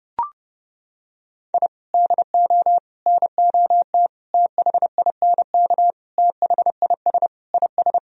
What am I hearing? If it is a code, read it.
Morse code, heard as a keyed tone, about 32 wpm, 700 Hz. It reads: I DO NOT THINK THIS IS
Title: Sentences from Top 1000 Words - Review Part 4 - 30wpm